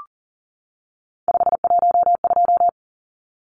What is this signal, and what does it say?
Morse code, heard as a keyed tone, about 40 wpm, 700 Hz. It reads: 512
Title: Numbers - 3-Digit 40wpm